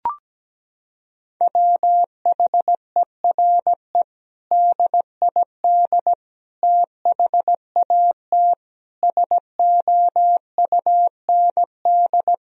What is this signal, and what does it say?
Morse code, heard as a keyed tone, about 17 wpm, 700 Hz. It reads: WHERE DID THAT SOUND
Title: Sentences from Top 1000 Words - Review Part 1 - 17wpm